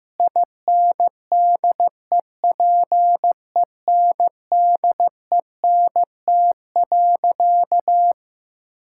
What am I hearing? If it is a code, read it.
Morse code, heard as a keyed tone, about 15 wpm, 700 Hz. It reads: INDEPENDENT.
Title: Sentences from Top 1000 Words - Review Part 6 - 15wpm